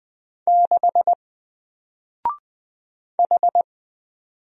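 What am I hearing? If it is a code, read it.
Morse code, heard as a keyed tone, about 20 wpm, 700 Hz. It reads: 6 H
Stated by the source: Single letter and number Morse drill at 20 wpm, from N4LCD.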